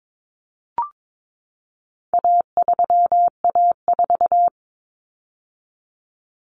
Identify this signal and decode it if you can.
Morse code, heard as a keyed tone, about 22 wpm, 700 Hz. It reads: A3A4
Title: Common Cut Numbers - 4-Digit 22wpm